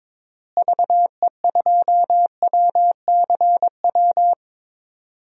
Morse code, heard as a keyed tone, about 21 wpm, 700 Hz. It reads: VE2WCW
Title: DXCC-100 Call Signs - 6 Characters - Repeated 3x - 22wpm